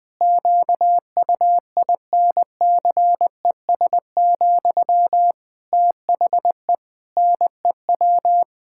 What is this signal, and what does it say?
Morse code, heard as a keyed tone, about 20 wpm, 700 Hz. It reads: QUINCES, THE NEW
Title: Pangram Challenge v2 - 20wpm